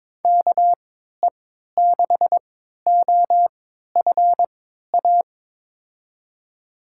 Morse code, heard as a keyed tone, about 22 wpm, spaced as for 13 wpm, 700 Hz. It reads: KE6OFA
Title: USA Call Signs - 6 Characters - 3x Character Spacing - 22wpm